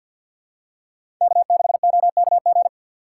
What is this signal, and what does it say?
Morse code, heard as a keyed tone, about 53 wpm, 700 Hz. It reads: K6OKG
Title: USA Call Signs - 5 Characters - 6x Character Spacing - 50wpm